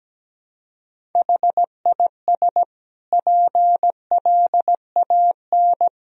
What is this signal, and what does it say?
Morse code, heard as a keyed tone, about 17 wpm, 700 Hz. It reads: HIS PLAN
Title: Sentences from Top 500 Words and Abbreviations 17wpm